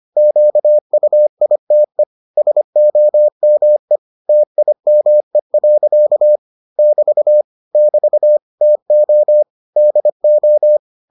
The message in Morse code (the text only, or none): QUITE SOME TIME. <BT> <BT> TO DO